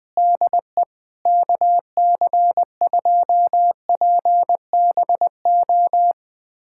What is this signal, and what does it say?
Morse code, heard as a keyed tone, about 20 wpm, 700 Hz. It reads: DE KC2PBO
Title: USA Callsign DE Callsign 20wpm